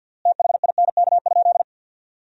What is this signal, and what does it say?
Morse code, heard as a keyed tone, about 53 wpm, 700 Hz. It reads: THINK?
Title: 3x Word-Spacing - Sentences from Top 200 Words - 50wpm